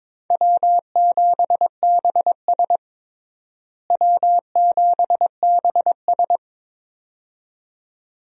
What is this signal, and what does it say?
Morse code, heard as a keyed tone, about 21 wpm, 700 Hz. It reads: W7BS W7BS
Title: USA Call Signs - 4 Characters - Repeated 2x - 22wpm